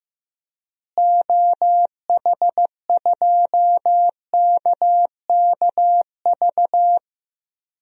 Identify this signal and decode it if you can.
Morse code, heard as a keyed tone, about 16 wpm, 700 Hz. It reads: OH2KKV
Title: DXCC-100 Call Signs - 6 Characters - Repeated 3x - 15wpm